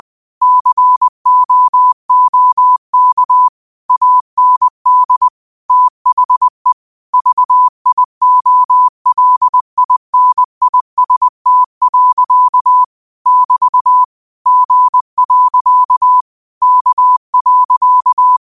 Morse code, heard as a keyed tone, about 20 wpm, 1000 Hz. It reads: COOK AND THE VIOLINIST. <BT> G. K.
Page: Quotes for Sat, 16 Aug 2025 in Morse Code at 20 words per minute.